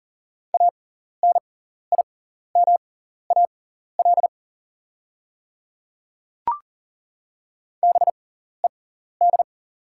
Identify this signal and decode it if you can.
Morse code, heard as a keyed tone, about 40 wpm, spaced as for 14 wpm, 700 Hz. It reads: ANIMAL BED
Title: 6x Character Spacing - Top 500 Words - 40wpm